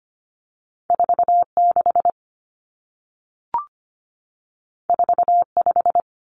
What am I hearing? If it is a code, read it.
Morse code, heard as a keyed tone, about 25 wpm, 700 Hz. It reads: 46 45
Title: Common Cut Numbers - 2-Digit 25wpm